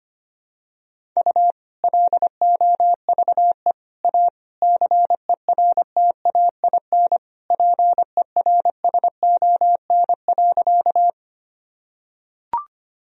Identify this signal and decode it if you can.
Morse code, heard as a keyed tone, about 25 wpm, 700 Hz. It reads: U LOVE A CERTAIN PERSON.
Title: Sentences from Top 500 Words and Abbreviations 25wpm